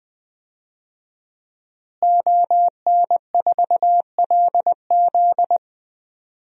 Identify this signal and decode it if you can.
Morse code, heard as a keyed tone, about 20 wpm, 700 Hz. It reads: ON4LZ
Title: DXCC-100 Call Signs - 5 Characters - Repeated 3x - 20wpm